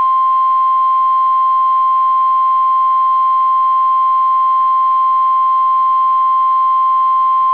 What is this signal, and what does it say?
Emergency Alert System (SAME) tones (the 1050 Hz attention tone). It is heard from 1.1 to 6.5 s.